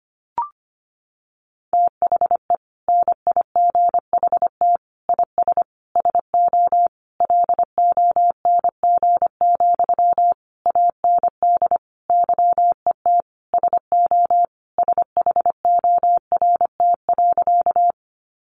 Morse code, heard as a keyed tone, about 25 wpm, 700 Hz. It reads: THE NIGHT IS SO LONG, AND YET SO SHORT.
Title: Sentences from Top 500 Words 25wpm